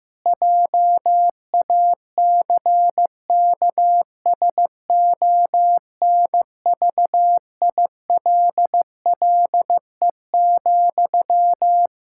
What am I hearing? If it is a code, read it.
Morse code, heard as a keyed tone, about 15 wpm, 700 Hz. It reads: JACKSONVILLE,
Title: Practice QSO - Exchange 1 - Beginning - 15wpm